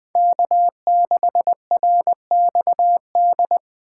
Morse code, heard as a keyed tone, about 25 wpm, 700 Hz. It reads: K6RXD
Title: Speed Racing - USA 1x3 Call Signs 20wpm